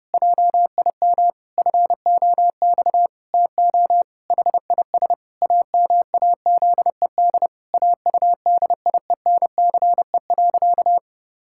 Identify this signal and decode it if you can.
Morse code, heard as a keyed tone, about 32 wpm, 700 Hz. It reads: JIM FOX TO HIS AMAZED AUDIENCE.
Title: Pangram Challenge v2 - 30wpm